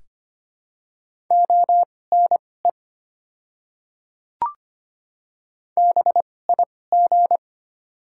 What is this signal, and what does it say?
Morse code, heard as a keyed tone, about 25 wpm, spaced as for 18 wpm, 700 Hz. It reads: ONE BIG
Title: 2x Character Spacing - Top 200 Words - 25wpm